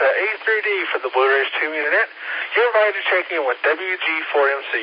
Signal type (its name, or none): Emergency Alert System (SAME) tones